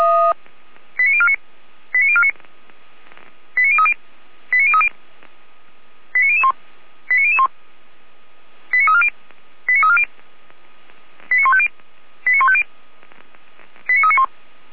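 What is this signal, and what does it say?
Five-tone selective call (ZVEI), heard as quick runs of five tones, tones of about 70 ms. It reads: 89049 89049 89030 89030 89001 89001 89340 89340 89150 89150 89391
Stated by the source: ZVEI five-tone call 89220 received on SDR, from a feuerwehr-orlamuende.org fire brigade audio server.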